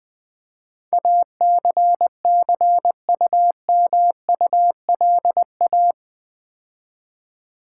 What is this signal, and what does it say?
Morse code, heard as a keyed tone, about 19 wpm, 700 Hz. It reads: ACCUMULA
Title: Build A Word - 4 Letters At A Time - 12-Letter Words - 20wpm